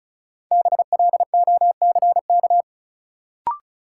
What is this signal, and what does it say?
Morse code, heard as a keyed tone, about 34 wpm, 700 Hz. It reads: BLOCK
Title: Sentences from Top 1600 Words - New Word Drill - 35wpm